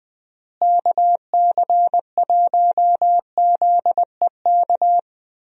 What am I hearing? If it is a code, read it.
Morse code, heard as a keyed tone, about 20 wpm, 700 Hz. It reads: KC1ZEK
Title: USA Call Signs - 6 Characters - Repeated 3x - 20wpm